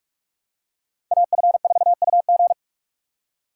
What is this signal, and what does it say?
Morse code, heard as a keyed tone, about 44 wpm, 700 Hz. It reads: AF4UG